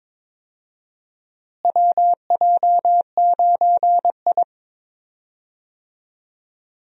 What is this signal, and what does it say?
Morse code, heard as a keyed tone, about 21 wpm, 700 Hz. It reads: WJ9I
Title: USA Call Signs - 4 Characters - Repeated 3x - 22wpm